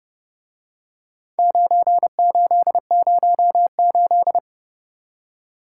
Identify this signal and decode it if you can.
Morse code, heard as a keyed tone, about 30 wpm, 700 Hz. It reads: 9808